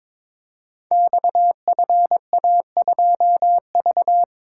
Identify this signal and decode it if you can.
Morse code, heard as a keyed tone, about 22 wpm, 690 Hz. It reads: XFA2V
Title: Code Groups - 5 Letter:Numbers - 22wpm